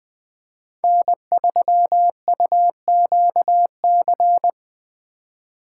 Morse code, heard as a keyed tone, about 20 wpm, 700 Hz. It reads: N3UQC
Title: Speed Racing - USA 1x3 Call Signs 17wpm